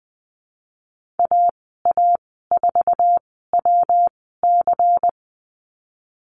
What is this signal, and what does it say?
Morse code, heard as a keyed tone, about 20 wpm, spaced as for 15 wpm, 700 Hz. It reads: AA4WC
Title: 219 Lesson 35g - Number 0 - Call Signs 20wpm